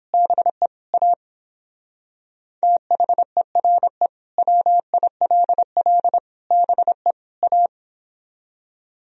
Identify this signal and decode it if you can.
Morse code, heard as a keyed tone, about 29 wpm, 700 Hz. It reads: BE A THERE WILL BE A